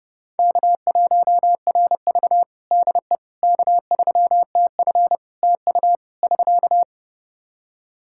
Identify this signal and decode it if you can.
Morse code, heard as a keyed tone, about 32 wpm, 700 Hz. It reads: K1RV DE K3TF TU <SK>
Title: Practice QSO - Exchange 4 - Ending - 30wpm